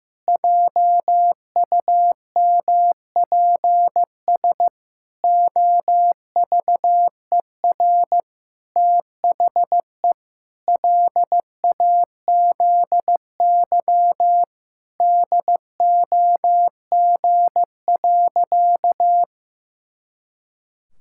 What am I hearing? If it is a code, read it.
Morse code, heard as a keyed tone, about 15 wpm, 700 Hz. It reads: JUMPS OVER THE LAZY DOG.
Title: Pangram Challenge v2 - 15wpm